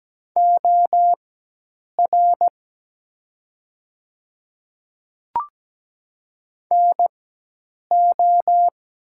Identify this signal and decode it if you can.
Morse code, heard as a keyed tone, about 17 wpm, spaced as for 8 wpm, 700 Hz. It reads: OR NO
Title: 4x Character Spacing - Top 100 Words - 17wpm